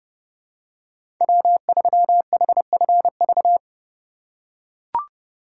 Morse code, heard as a keyed tone, about 31 wpm, 700 Hz. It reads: W3HFV
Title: USA Call Signs - 5 Characters - 2x Character Spacing - 30wpm